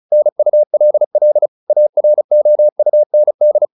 Morse code, heard as a keyed tone, about 34 wpm, 600 Hz. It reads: NULL AROUND